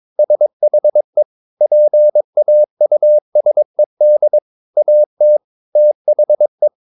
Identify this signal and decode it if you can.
Morse code, heard as a keyed tone, about 22 wpm, 600 Hz. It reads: SHE PAUSED AT THE
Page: Romanen Never af Ken Follet på engelsk.